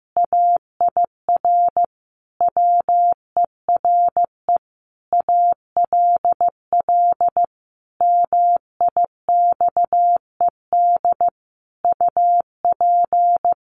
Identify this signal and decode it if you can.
Morse code, heard as a keyed tone, about 15 wpm, 700 Hz. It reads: AIR WERE ALL MIXED UP
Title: Sentences from Top 1500 Words 15wpm